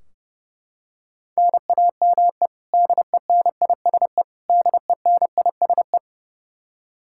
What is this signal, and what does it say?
Morse code, heard as a keyed tone, about 30 wpm, 700 Hz. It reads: NAME DENISE DENISE
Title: Practice QSO - Exchange 1 - Beginning - 30wpm